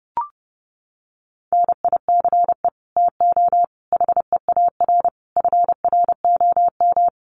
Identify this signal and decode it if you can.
Morse code, heard as a keyed tone, about 30 wpm, 700 Hz. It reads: NICE TO HEAR FROM
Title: Common QSO Phrases 30wpm